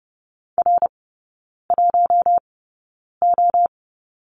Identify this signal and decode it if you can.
Morse code, heard as a keyed tone, about 30 wpm, spaced as for 9 wpm, 700 Hz. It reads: R1O
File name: Lesson 16e - Letter c - Three Characters 30wpm